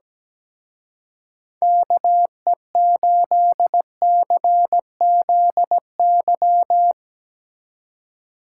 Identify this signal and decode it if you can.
Morse code, heard as a keyed tone, about 18 wpm, 700 Hz. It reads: KE8CZY
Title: Speed Racing - USA 2x3 Call Signs 17wpm